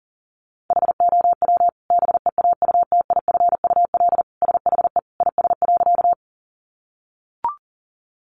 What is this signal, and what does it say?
Morse code, heard as a keyed tone, about 40 wpm, 700 Hz. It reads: HOW BEAUTIFUL SHE IS.
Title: Sentences from Top 900 Words 40wpm